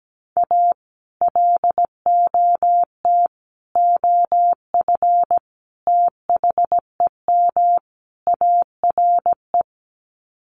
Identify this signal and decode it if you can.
Morse code, heard as a keyed tone, about 17 wpm, 700 Hz. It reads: A LOT OF THEM ARE
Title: Sets of 5 Words Encore2 17wpm